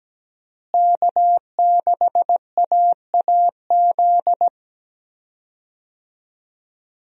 Morse code, heard as a keyed tone, about 17 wpm, 700 Hz. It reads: K6AAZ
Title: USA Call Signs - 5 Characters - Repeated 3x - 17wpm